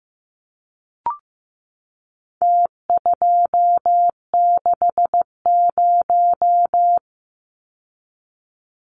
Morse code, heard as a keyed tone, about 15 wpm, 700 Hz. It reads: T260
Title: Numbers - Contest Style - 4-Digit 15wpm